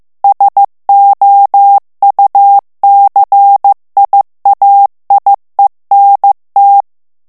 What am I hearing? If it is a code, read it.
Morse code, heard as a keyed tone, about 15 wpm, 800 Hz. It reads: SOUCIAIENT
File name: morse mp3 / FRANCAIS TEXTE CLAIR 2 VITESSE 14 MOTS MINUTES